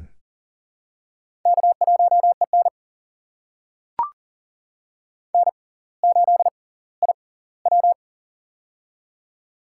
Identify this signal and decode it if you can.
Morse code, heard as a keyed tone, about 40 wpm, spaced as for 20 wpm, 700 Hz. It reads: K1EN N8IW
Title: USA Call Signs - 4 Characters - 6x Character Spacing - 40wpm